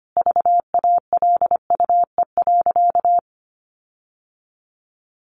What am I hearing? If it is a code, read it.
Morse code, heard as a keyed tone, about 25 wpm, 700 Hz. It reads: VALUE.
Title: Sentences from Top 1300 Words 25wpm